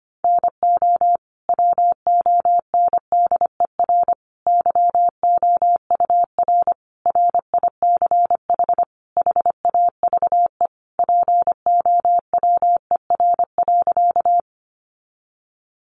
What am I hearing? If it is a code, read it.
Morse code, heard as a keyed tone, about 25 wpm, 700 Hz. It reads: NO WONDER YOUR RICH HAVE POWER.
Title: Sentences from Top 1100 Words 25wpm